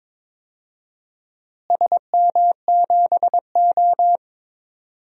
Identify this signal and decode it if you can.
Morse code, heard as a keyed tone, about 22 wpm, 700 Hz. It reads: SM7O
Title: DXCC-100 Call Signs - 4 Characters - Repeated 3x - 22wpm